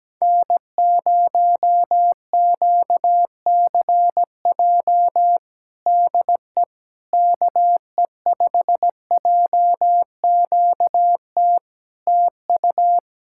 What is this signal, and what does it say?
Morse code, heard as a keyed tone, about 17 wpm, 700 Hz. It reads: N0QCJ DE KE5JQT TU
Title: Practice QSO - Exchange 4 - Ending - 17wpm